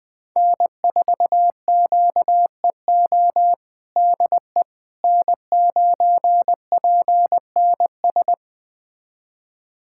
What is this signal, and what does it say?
Morse code, heard as a keyed tone, about 21 wpm, 700 Hz. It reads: N4QEO DE N9PNS
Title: USA Callsign DE Callsign - 5 Characters - 20wpm